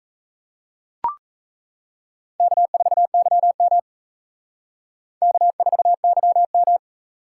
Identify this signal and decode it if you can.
Morse code, heard as a keyed tone, about 42 wpm, 700 Hz. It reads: K4YM K4YM